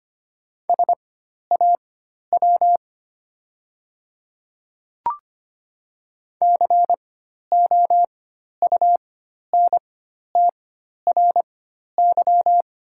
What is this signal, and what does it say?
Morse code, heard as a keyed tone, about 25 wpm, spaced as for 12 wpm, 700 Hz. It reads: SAW COUNTRY
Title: 4x Character Spacing - Top 300 Words - 25wpm